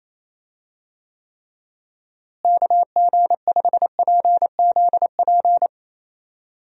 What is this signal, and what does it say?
Morse code, heard as a keyed tone, about 28 wpm, 700 Hz. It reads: KG5PZP
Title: USA Call Signs - 6 Characters - Repeated 3x - 28wpm